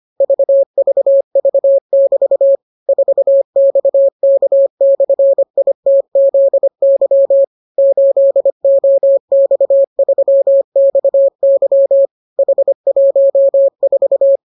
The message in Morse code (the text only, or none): VVV<BT> 4XK/ITZY 8OX3XY H14